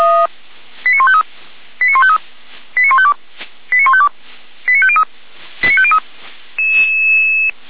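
Five-tone selective call (ZVEI), heard as quick runs of five tones, tones of about 70 ms. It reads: 89153 89153 89152 89152 89503 89503
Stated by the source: Received by SDR, ZVEI five-tone selective call 89150, fire brigade Feuerwehr Orlamünde.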